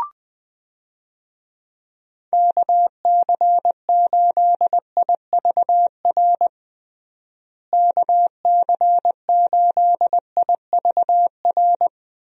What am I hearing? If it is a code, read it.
Morse code, heard as a keyed tone, about 21 wpm, 700 Hz. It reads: KC8IVR KC8IVR
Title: USA Call Signs - 6 Characters - Repeated 2x - 20wpm